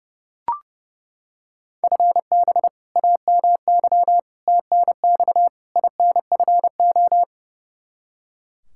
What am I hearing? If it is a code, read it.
Morse code, heard as a keyed tone, about 30 wpm, 700 Hz. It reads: FB AMY TNX INFO
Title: Practice QSO - Exchange 3 - More Info - 30wpm